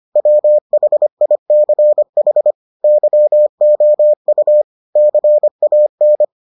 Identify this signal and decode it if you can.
Morse code, heard as a keyed tone, about 25 wpm, 600 Hz. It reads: WHICH YOU CAN